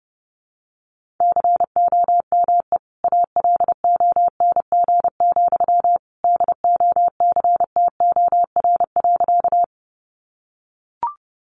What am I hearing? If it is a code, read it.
Morse code, heard as a keyed tone, about 30 wpm, 700 Hz. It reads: COME ALONG, DOCTOR.
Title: Sentences from Top 1500 Words 30wpm